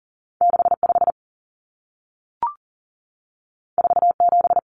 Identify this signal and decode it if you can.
Morse code, heard as a keyed tone, about 40 wpm, 700 Hz. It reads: 65 47
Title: Numbers - 2-Digit 40wpm